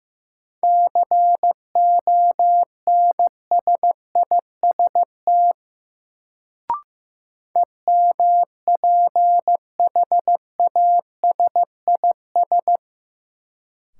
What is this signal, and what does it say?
Morse code, heard as a keyed tone, about 15 wpm, 700 Hz. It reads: CONSIST EMPHASIS
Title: Sentences from Top 1400 Words - New Word Drill - 15wpm